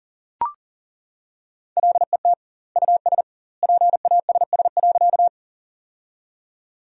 Morse code, heard as a keyed tone, about 40 wpm, 700 Hz. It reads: LET US PASS.
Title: 2x Word-Spacing - Sentences from Top 400 Words - 40wpm